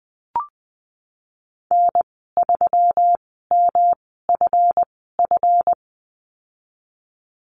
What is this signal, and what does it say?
Morse code, heard as a keyed tone, about 20 wpm, spaced as for 15 wpm, 700 Hz. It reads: N3MFF
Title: 179 Lesson 29g - Slash - Call Signs 20wpm